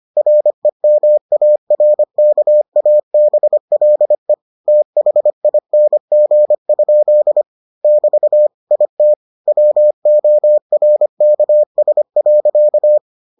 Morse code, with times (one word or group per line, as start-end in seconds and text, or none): 0.2-4.3 REMARKABLE
4.7-7.4 THING?
7.8-8.5 <BT>
8.7-9.1 IT
9.5-13.0 WORKS.